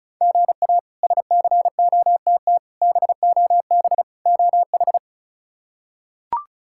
Morse code, heard as a keyed tone, about 36 wpm, 700 Hz. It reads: GA SCOTT BOB OH
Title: K1USN SST QSO Practice - 35wpm